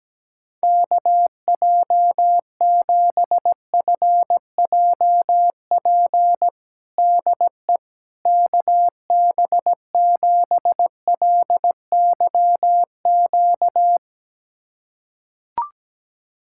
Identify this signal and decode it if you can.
Morse code, heard as a keyed tone, about 17 wpm, 700 Hz. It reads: KJ7FJP DE KB7LYQ
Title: USA Callsign DE Callsign - 6 Characters - 17wpm